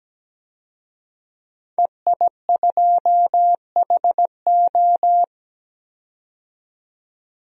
Morse code, heard as a keyed tone, about 17 wpm, 700 Hz. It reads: EI2HO
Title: DXCC-100 Call Signs - 5 Characters - 17wpm